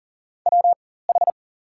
Morse code, heard as a keyed tone, about 40 wpm, spaced as for 19 wpm, 700 Hz. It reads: WH
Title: 4x Character Spacing - Top 200 Words - 40wpm